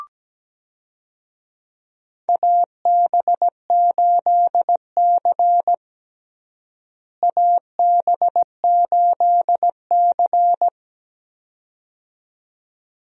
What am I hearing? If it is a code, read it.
Morse code, heard as a keyed tone, about 17 wpm, 700 Hz. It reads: AB8C AB8C
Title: USA Call Signs - 4 Characters - Repeated 2x - 17wpm